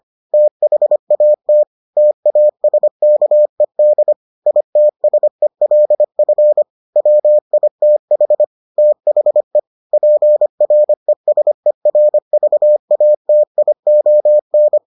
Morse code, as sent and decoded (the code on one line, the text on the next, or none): - .... .- - / - .- ... -.- . -.. / .. - ... .-.. ..-. / .-- .. - .... / - .... / .--. .-. ... .-. ...- .- - .. --- -.
THAT TASKED ITSELF WITH THE PRESERVATION